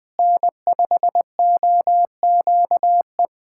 Morse code, heard as a keyed tone, about 20 wpm, 700 Hz. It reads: N5OQE
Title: USA Callsign DE Callsign 20wpm